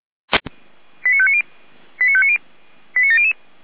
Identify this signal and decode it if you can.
Five-tone selective call (ZVEI), heard as quick runs of five tones, tones of about 70 ms. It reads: 89590 89590 89700
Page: /455/SDR/ZVEI/